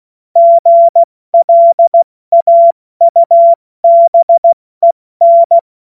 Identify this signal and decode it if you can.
Morse code, heard as a keyed tone, about 16 wpm, 680 Hz. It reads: GLAUBEN
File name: Morsekurs / Witze / Witz 27. Juli 2021, 70 F80